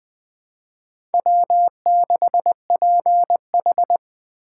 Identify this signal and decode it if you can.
Morse code, heard as a keyed tone, about 20 wpm, 700 Hz. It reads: W6PH
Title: USA Call Signs - 4 Characters - 2x Character Spacing - 20wpm